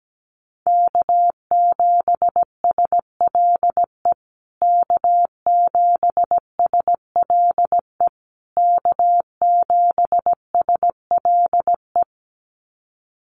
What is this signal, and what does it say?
Morse code, heard as a keyed tone, about 17 wpm, 700 Hz. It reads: K7SLE K7SLE K7SLE
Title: USA Call Signs Repeated 3x 17wpm